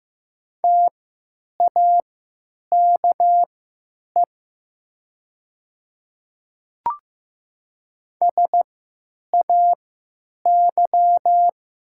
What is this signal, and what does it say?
Morse code, heard as a keyed tone, about 15 wpm, spaced as for 9 wpm, 700 Hz. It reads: TAKE SAY
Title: 3x Character Spacing - Top 500 Words - 15wpm